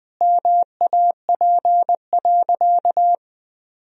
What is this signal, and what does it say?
Morse code, heard as a keyed tone, about 19 wpm, 700 Hz. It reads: MAP.
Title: Sentences from Top 1000 Words - Review Part 2 - 20wpm